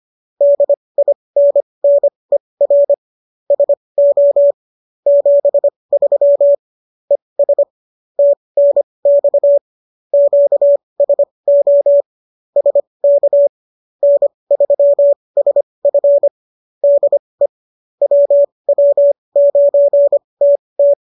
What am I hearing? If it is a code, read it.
Morse code, heard as a keyed tone, about 25 wpm, spaced as for 20 wpm, 570 Hz. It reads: DINNER SO 73 ES TNX QSO SK N3SF DE WW9TT